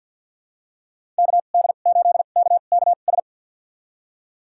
Morse code, heard as a keyed tone, about 48 wpm, spaced as for 35 wpm, 690 Hz. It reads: KD8KKS
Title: USA Call Signs - 2x Character Spacing - 50wpm